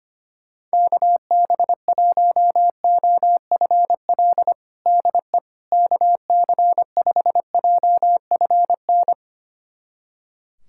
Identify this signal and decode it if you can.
Morse code, heard as a keyed tone, about 25 wpm, 700 Hz. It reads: KB1OFL DE KC5JFN
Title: USA Callsign DE Callsign - 6 Characters - 25wpm